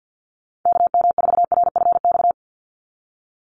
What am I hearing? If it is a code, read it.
Morse code, heard as a keyed tone, about 48 wpm, 700 Hz. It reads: KM4RFX
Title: USA Call Signs 50wpm